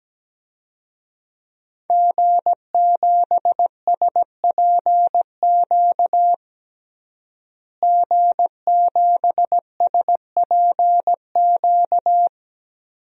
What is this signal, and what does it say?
Morse code, heard as a keyed tone, about 17 wpm, 700 Hz. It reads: G7SPQ G7SPQ
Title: DXCC-100 Call Signs - 5 Characters - Repeated 2x - 17wpm